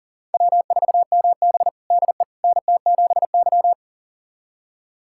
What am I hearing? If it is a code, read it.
Morse code, heard as a keyed tone, about 40 wpm, 700 Hz. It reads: W4MB DE NT7Y
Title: USA Callsign DE Callsign - 4 Characters - 40wpm